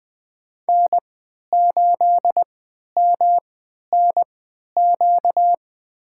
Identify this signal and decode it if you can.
Morse code, heard as a keyed tone, about 21 wpm, spaced as for 12 wpm, 700 Hz. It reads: N8MNQ
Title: USA Call Signs - 3x Character Spacing - 20wpm